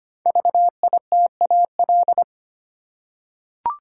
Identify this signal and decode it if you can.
Morse code, heard as a keyed tone, about 25 wpm, 700 Hz. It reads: VITAL
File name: 5 Letter Words - First 3 Letters Given - 25wpm